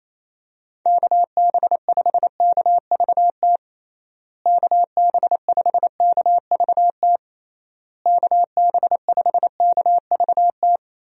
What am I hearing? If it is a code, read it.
Morse code, heard as a keyed tone, about 28 wpm, 700 Hz. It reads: KB5KVT KB5KVT KB5KVT
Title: USA Call Signs - 6 Characters - Repeated 3x - 28wpm